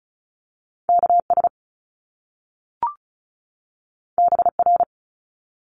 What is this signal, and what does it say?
Morse code, heard as a keyed tone, about 36 wpm, 700 Hz. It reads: KS BR
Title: Top Two Letter Combinations in Words 35wpm